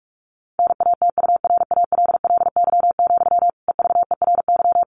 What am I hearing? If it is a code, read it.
Morse code, heard as a keyed tone, about 44 wpm, 700 Hz. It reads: NATURALLY, EVERY
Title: Sentences from Top 1500 Words 45wpm